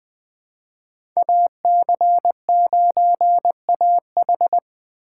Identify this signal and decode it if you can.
Morse code, heard as a keyed tone, about 20 wpm, 700 Hz. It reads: AC9AH
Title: USA Call Signs - 5 Characters - 4x Character Spacing - 20wpm